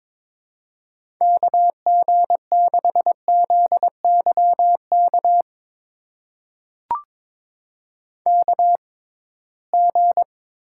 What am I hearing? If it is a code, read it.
Morse code, heard as a keyed tone, about 22 wpm, spaced as for 17 wpm, 700 Hz. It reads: KG6ZYK KG
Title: USA Call Signs - 6 Characters - 6x Character Spacing - 22wpm